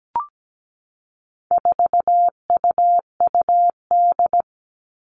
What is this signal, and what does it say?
Morse code, heard as a keyed tone, about 17 wpm, 700 Hz. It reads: 4UUD
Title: All Cut Numbers - 4-Digit 17wpm